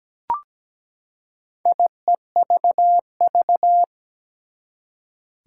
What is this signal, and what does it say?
Morse code, heard as a keyed tone, about 17 wpm, 700 Hz. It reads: IEVV
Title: Four Challenging Letter-Number Combinations 17wpm